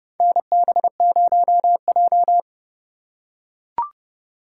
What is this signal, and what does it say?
Morse code, heard as a keyed tone, about 30 wpm, 700 Hz. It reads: NB0J
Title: USA Call Signs - 4 Characters - 2x Character Spacing - 30wpm